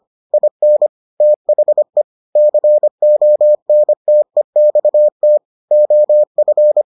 Morse code, heard as a keyed tone, about 25 wpm, 600 Hz. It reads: IN THE CONTEXT OF